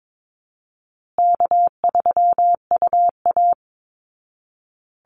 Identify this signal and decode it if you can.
Morse code, heard as a keyed tone, about 22 wpm, 700 Hz. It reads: K3UA
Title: Easy Call Signs Repeated 3x - 22wpm